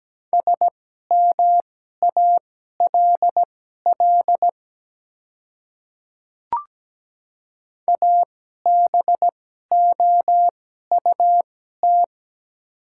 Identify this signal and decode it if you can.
Morse code, heard as a keyed tone, about 17 wpm, spaced as for 12 wpm, 700 Hz. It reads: SMALL ABOUT
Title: 2x Character Spacing - Top 300 Words - 17wpm